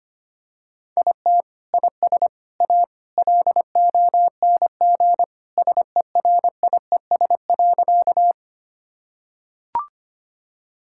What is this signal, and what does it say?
Morse code, heard as a keyed tone, about 25 wpm, 700 Hz. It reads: IT IS A LONG SERIES.
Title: Sentences from Top 1000 Words - Review Part 5 - 25wpm